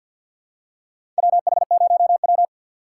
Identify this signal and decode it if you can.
Morse code, heard as a keyed tone, about 50 wpm, 700 Hz. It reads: WH0W
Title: USA Call Signs - 4x Character Spacing - 50wpm